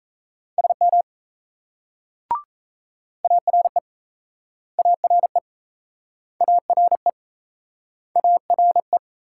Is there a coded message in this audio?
Yes: Morse code from 0.6 to 9.0 s, heard as a keyed tone, about 38 wpm, 700 Hz. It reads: SM ARE ARE ARE ARE